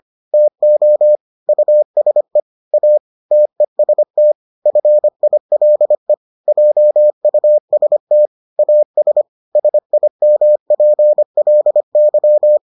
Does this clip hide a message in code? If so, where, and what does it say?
Yes: Morse code from 0.3 to 12.6 s, heard as a keyed tone, about 25 wpm, 600 Hz. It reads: TO USE A TEST FILE JUST AS SIMPLY